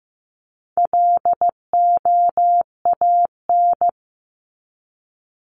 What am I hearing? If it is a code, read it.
Morse code, heard as a keyed tone, about 15 wpm, 700 Hz. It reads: LOAN
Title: Top 1501-1750 Words 15wpm